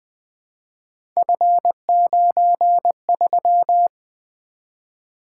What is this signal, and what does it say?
Morse code, heard as a keyed tone, about 20 wpm, 700 Hz. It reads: F93
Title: Three Letter-Number Combinations - 20wpm